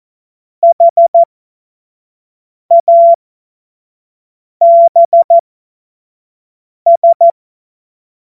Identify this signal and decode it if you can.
Morse code, heard as a keyed tone, about 14 wpm, spaced as for 5 wpm, 680 Hz. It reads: HABS
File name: Morsekurs / Witze / Witz 14. Februar 2023 25 F70